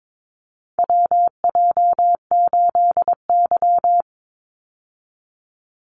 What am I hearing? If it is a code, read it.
Morse code, heard as a keyed tone, about 22 wpm, 700 Hz. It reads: WJ8Y
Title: Call Signs Repeated 2x - 22wpm